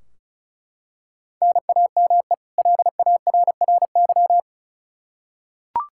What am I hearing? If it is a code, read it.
Morse code, heard as a keyed tone, about 34 wpm, 700 Hz. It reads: NAME LARRY
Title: Practice QSO - Exchange 1 - Beginning - 35wpm